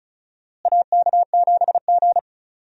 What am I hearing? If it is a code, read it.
Morse code, heard as a keyed tone, about 34 wpm, 700 Hz. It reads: AK7G
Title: USA Call Signs - 4 Characters - Repeated 2x - 35wpm